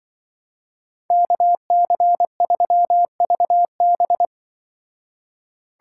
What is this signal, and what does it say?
Morse code, heard as a keyed tone, about 24 wpm, 700 Hz. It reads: KC3VB
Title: Speed Racing - USA 2x2 Call Signs 20wpm